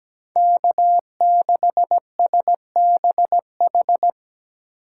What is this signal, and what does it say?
Morse code, heard as a keyed tone, about 20 wpm, 700 Hz. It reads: K6SBH
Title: Speed Racing - USA Call Signs 17wpm